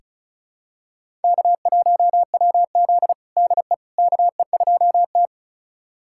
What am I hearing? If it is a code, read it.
Morse code, heard as a keyed tone, about 36 wpm, 700 Hz. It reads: K1WZ DE KE2T
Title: USA Callsign DE Callsign - 4 Characters - 35wpm